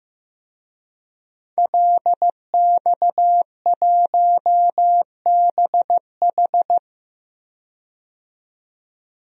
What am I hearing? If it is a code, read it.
Morse code, heard as a keyed tone, about 15 wpm, 700 Hz. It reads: LX1BH
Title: DXCC-100 Call Signs - 5 Characters - 15wpm